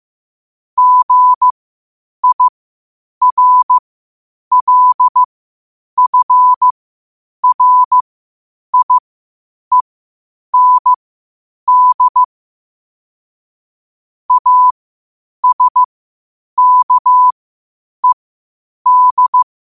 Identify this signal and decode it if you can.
Morse code, heard as a keyed tone, about 15 wpm, spaced as for 8 wpm, 1000 Hz. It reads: GIRLFRIEND ASKED
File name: qotdfiles / 8wpm